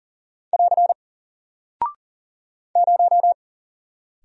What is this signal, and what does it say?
Morse code, heard as a keyed tone, about 39 wpm, 700 Hz. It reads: <AR> 0
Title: Single Character (Letter:Number:Punctuation:Prosign) Alt 40wpm